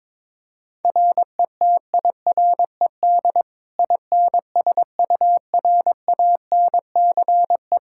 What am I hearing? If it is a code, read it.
Morse code, heard as a keyed tone, about 22 wpm, 700 Hz. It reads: RETIRED INSURANCE
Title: Practice QSO - Exchange 3 - More Info - 22wpm